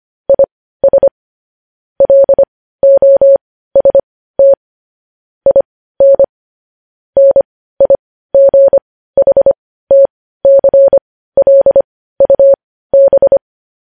Morse code, heard as a keyed tone, about 25 wpm, spaced as for 15 wpm, 550 Hz. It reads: IS LOST IN NIGHTCLUB
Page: news-headlines-in-morse-code-at-15-wpm